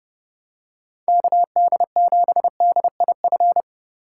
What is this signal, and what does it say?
Morse code, heard as a keyed tone, about 29 wpm, 700 Hz. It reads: KD7DIF